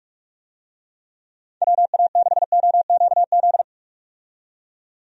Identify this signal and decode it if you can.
Morse code, heard as a keyed tone, about 46 wpm, 700 Hz. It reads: WA6OQZ
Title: USA Call Signs - 6 Characters - 45wpm